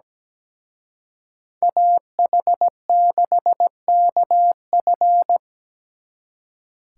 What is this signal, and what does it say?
Morse code, heard as a keyed tone, about 20 wpm, 700 Hz. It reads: AH6KF
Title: Speed Racing - USA 2x2 Call Signs 17wpm